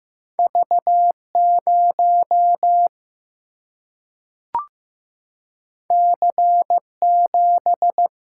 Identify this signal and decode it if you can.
Morse code, heard as a keyed tone, about 15 wpm, 700 Hz. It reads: V 0 C 7
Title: Single-Letter Single-Number 15wpm